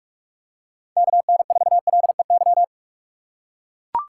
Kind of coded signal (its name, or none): Morse code